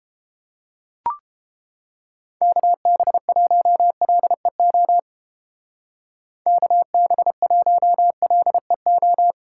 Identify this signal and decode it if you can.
Morse code, heard as a keyed tone, about 31 wpm, 700 Hz. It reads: KB1LEO KB1LEO